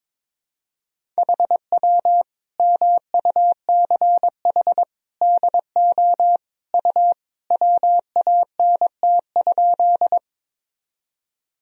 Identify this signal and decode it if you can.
Morse code, heard as a keyed tone, about 22 wpm, 700 Hz. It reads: HW MUCH DO U WANT?
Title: Sentences from Top 200 Words and Abbreviations 22wpm